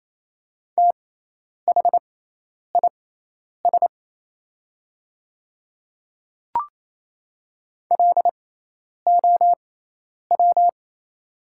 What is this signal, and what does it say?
Morse code, heard as a keyed tone, about 28 wpm, spaced as for 10 wpm, 700 Hz. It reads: THIS LOW
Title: 6x Character Spacing - Top 200 Words - 28wpm